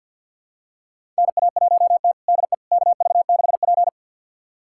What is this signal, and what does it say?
Morse code, heard as a keyed tone, about 49 wpm, 700 Hz. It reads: NA1T DE KV6P
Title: USA Callsign DE Callsign - 4 Characters - 50wpm